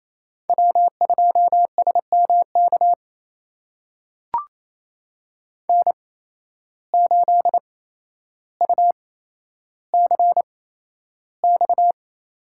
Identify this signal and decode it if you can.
Morse code, heard as a keyed tone, about 28 wpm, spaced as for 12 wpm, 700 Hz. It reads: W2SMK N8UCX
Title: USA Call Signs - 5 Characters - 8x Character Spacing - 28wpm